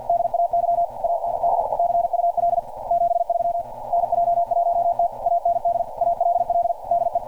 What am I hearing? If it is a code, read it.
Morse code, heard as a keyed tone, about 48 wpm, 690 Hz. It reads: DAY AND CUL 73 NJ1T AND ALL DE
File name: archived / CRSnet